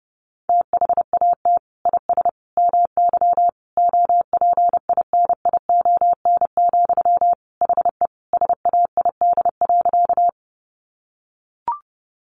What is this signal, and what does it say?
Morse code, heard as a keyed tone, about 30 wpm, 700 Hz. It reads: THAT IS MY OPINION, HE SAID.
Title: Sentences from Top 1200 Words 30wpm